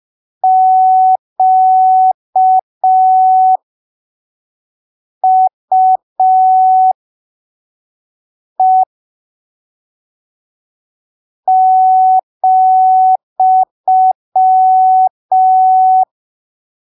Morse code, heard as a keyed tone, about 5 wpm, spaced as for 3 wpm, 750 Hz. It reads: QUE ,
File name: arquivos / texto / texto 05 5ppm